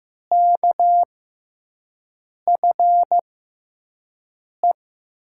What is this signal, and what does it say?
Morse code, heard as a keyed tone, about 15 wpm, spaced as for 5 wpm, 700 Hz. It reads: KFE
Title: USA Call Signs - 5 Characters - 6x Character Spacing - 15wpm